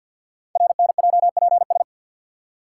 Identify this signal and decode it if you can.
Morse code, heard as a keyed tone, about 48 wpm, 700 Hz. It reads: RNJPS
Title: Code Groups - 5 Letters - 50wpm